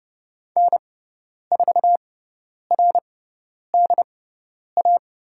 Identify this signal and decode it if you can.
Morse code, heard as a keyed tone, about 30 wpm, spaced as for 10 wpm, 700 Hz. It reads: N4RDA
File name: Lesson 18g - Letter m - Call Signs 30wpm